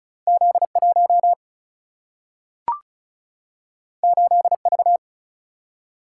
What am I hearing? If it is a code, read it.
Morse code, heard as a keyed tone, about 36 wpm, 700 Hz. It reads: Z 1 8 V
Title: Single-Letter Single-Number 35wpm